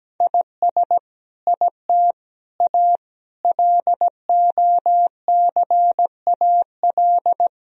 Morse code, heard as a keyed tone, about 17 wpm, 700 Hz. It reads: IS IT A LOCAL
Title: Sentences from Top 1000 Words - Review Part 3 - 17wpm